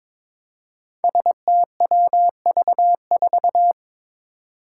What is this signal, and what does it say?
Morse code, heard as a keyed tone, about 22 wpm, 700 Hz. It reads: STWV4
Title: Five Challenging Letter-Number Combinations 22wpm